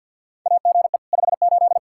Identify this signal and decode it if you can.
Morse code, heard as a keyed tone, about 52 wpm, 700 Hz. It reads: AGE 58
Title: Practice QSO - Exchange 3 - More Info - 50wpm